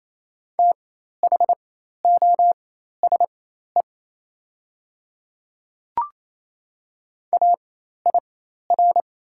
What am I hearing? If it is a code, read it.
Morse code, heard as a keyed tone, about 28 wpm, spaced as for 13 wpm, 690 Hz. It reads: THOSE AIR
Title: 4x Character Spacing - Top 500 Words - 28wpm